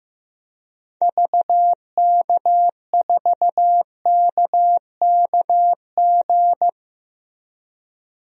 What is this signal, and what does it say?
Morse code, heard as a keyed tone, about 15 wpm, 700 Hz. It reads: VK4KKG
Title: DXCC-100 Call Signs - 6 Characters - Repeated 2x - 15wpm